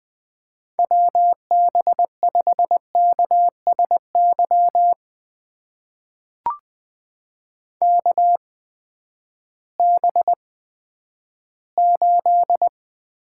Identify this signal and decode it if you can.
Morse code, heard as a keyed tone, about 20 wpm, spaced as for 11 wpm, 700 Hz. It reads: WB5KSY KB8
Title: USA Call Signs - 6 Characters - 8x Character Spacing - 20wpm